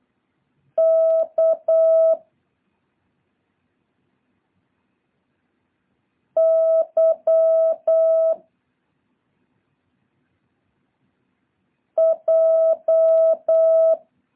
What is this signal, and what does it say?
Morse code, heard as a keyed tone, about 8 wpm, 650 Hz. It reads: K Y J
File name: jh8cbh / 2023cw